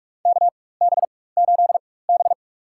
Morse code, heard as a keyed tone, about 46 wpm, spaced as for 21 wpm, 700 Hz. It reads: KB8B
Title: USA Call Signs - 4x Character Spacing - 45wpm